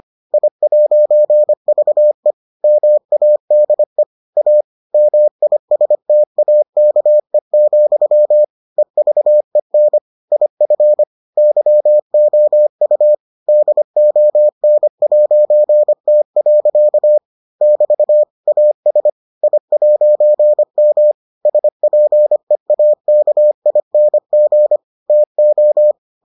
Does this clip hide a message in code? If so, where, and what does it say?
Yes: Morse code from 0.3 to 25.9 s, heard as a keyed tone, about 25 wpm, 600 Hz. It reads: I'VE MADE A MISTAKE, EVEN IF YOU DON'T. <BT> AS I'M SPEAKING TO